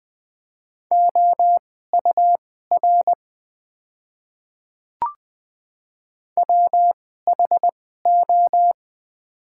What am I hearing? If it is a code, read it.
Morse code, heard as a keyed tone, about 21 wpm, spaced as for 15 wpm, 700 Hz. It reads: OUR WHO
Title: 2x Character Spacing - Top 200 Words - 20wpm